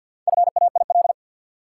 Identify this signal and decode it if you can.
Morse code, heard as a keyed tone, about 50 wpm, 700 Hz. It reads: FAIL
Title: Sentences from Top 1100 Words - New Word Drill - 50wpm